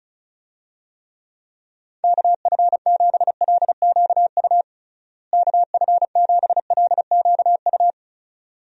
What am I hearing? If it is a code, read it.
Morse code, heard as a keyed tone, about 36 wpm, 700 Hz. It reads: KF7LQU KF7LQU
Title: USA Call Signs - 6 Characters - Repeated 2x - 35wpm